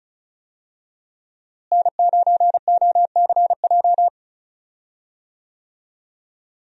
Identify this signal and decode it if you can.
Morse code, heard as a keyed tone, about 34 wpm, 700 Hz. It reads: N9OCJ
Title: USA Call Signs - 5 Characters - Repeated 3x - 35wpm